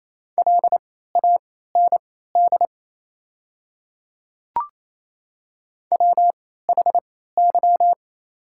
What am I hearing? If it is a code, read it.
Morse code, heard as a keyed tone, about 28 wpm, spaced as for 16 wpm, 700 Hz. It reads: LAND WHY
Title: 3x Character Spacing - Top 200 Words - 28wpm